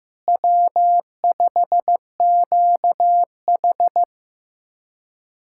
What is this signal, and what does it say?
Morse code, heard as a keyed tone, about 16 wpm, 700 Hz. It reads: W5QH
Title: USA Callsign DE Callsign - 4 Characters - 15wpm